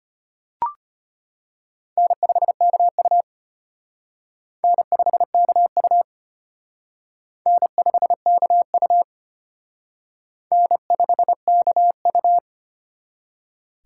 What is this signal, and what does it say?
Morse code, heard as a keyed tone, about 31 wpm, 700 Hz. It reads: N5KU N5KU N5KU N5KU
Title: Speed Racing - USA Call Signs 25wpm